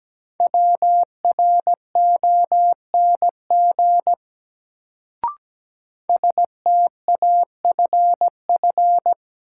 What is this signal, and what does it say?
Morse code, heard as a keyed tone, about 17 wpm, 700 Hz. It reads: WRONG STAFF
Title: Sets of 5 Words Encore - New Word Drill - 17wpm